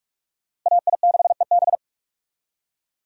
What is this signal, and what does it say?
Morse code, heard as a keyed tone, about 44 wpm, 700 Hz. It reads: AI6EB
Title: USA Call Signs - 5 Characters - Repeated 3x - 45wpm